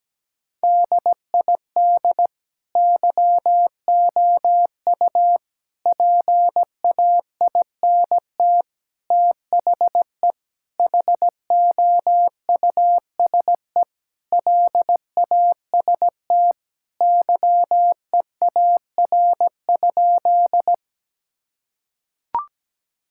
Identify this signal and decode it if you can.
Morse code, heard as a keyed tone, about 17 wpm, 700 Hz. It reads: DID YOU PAINT THE HOUSE LAST YEAR?
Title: Sentences from Top 1000 Words - Review Part 6 - 17wpm